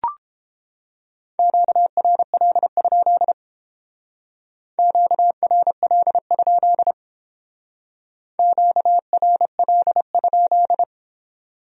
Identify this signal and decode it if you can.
Morse code, heard as a keyed tone, about 31 wpm, 700 Hz. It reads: QRL? QRL? QRL?